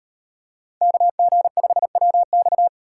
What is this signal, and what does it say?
Morse code, heard as a keyed tone, about 38 wpm, 700 Hz. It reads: KG5WX